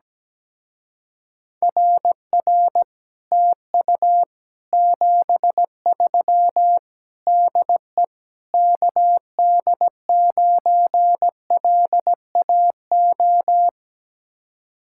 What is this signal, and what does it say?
Morse code, heard as a keyed tone, about 17 wpm, 700 Hz. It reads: RR TU 73 DE KD9LAO
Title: Speed-Racing Common QSO Phrases - 17wpm